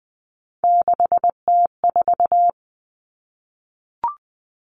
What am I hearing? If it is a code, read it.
Morse code, heard as a keyed tone, about 19 wpm, 700 Hz. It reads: 6T4
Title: Three Challenging Letter-Number Combinations 20wpm